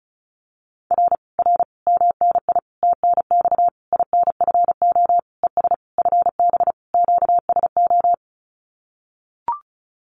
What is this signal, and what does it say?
Morse code, heard as a keyed tone, about 34 wpm, 700 Hz. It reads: R R MNI TNX INFO ES FB QSO
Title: Common QSO Phrases Encore 35wpm